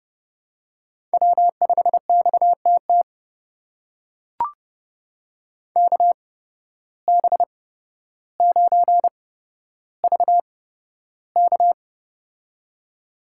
Morse code, heard as a keyed tone, about 31 wpm, spaced as for 13 wpm, 700 Hz. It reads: W5XTT KB9VK
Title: USA Call Signs - 5 Characters - 8x Character Spacing - 30wpm